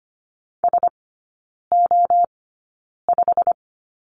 Morse code, heard as a keyed tone, about 25 wpm, spaced as for 9 wpm, 700 Hz. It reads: SO5
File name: Lesson 15e - Number 5 - Three Characters 25wpm